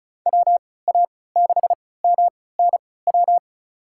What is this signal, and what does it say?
Morse code, heard as a keyed tone, about 36 wpm, spaced as for 20 wpm, 700 Hz. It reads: WA6MNW
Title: USA Call Signs - 6 Characters - 3x Character Spacing - 35wpm